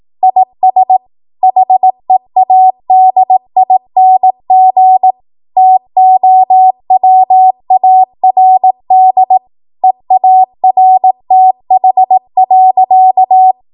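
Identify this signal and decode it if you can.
Morse code, heard as a keyed tone, about 18 wpm, 750 Hz. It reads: IS HEADING TOWARD EARTH.
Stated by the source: 18 WPM Morse code practice from a national amateur radio association's code archive.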